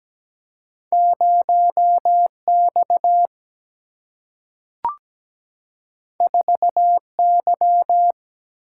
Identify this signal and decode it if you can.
Morse code, heard as a keyed tone, about 17 wpm, 700 Hz. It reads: 0 X 4 Y
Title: Single-Letter Single-Number 17wpm